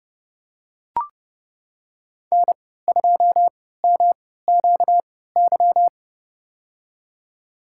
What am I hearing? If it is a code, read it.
Morse code, heard as a keyed tone, about 32 wpm, spaced as for 17 wpm, 700 Hz. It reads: N2MQY
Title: USA Call Signs - 3x Character Spacing - 30wpm